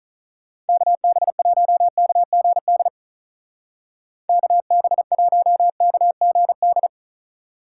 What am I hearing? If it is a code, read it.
Morse code, heard as a keyed tone, about 47 wpm, 700 Hz. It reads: KB1KGD KB1KGD